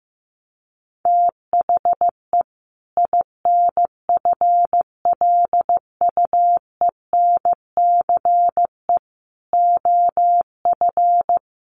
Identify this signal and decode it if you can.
Morse code, heard as a keyed tone, about 15 wpm, 700 Hz. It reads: THE INFLUENCE OF
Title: Sentences from Top 1100 Words 15wpm